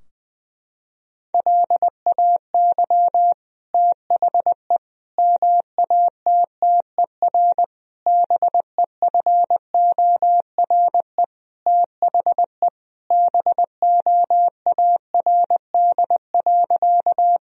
Morse code, heard as a keyed tone, about 20 wpm, 700 Hz. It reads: LAY THE MATTER BEFORE THE BOARD.
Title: Sentences from Top 1000 Words - Review Part 4 - 20wpm